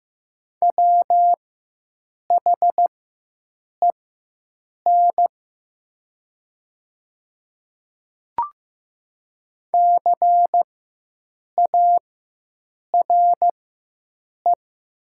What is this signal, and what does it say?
Morse code, heard as a keyed tone, about 15 wpm, spaced as for 7 wpm, 700 Hz. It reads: WHEN CARE
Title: 4x Character Spacing - Top 300 Words - 15wpm